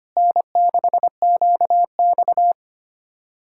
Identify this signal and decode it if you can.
Morse code, heard as a keyed tone, about 32 wpm, 700 Hz. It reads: N6QX